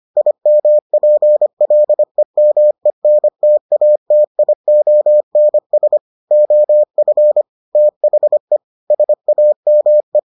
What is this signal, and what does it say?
Morse code, heard as a keyed tone, about 25 wpm, 600 Hz. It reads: IMPLEMENTATIONS OF THE SAME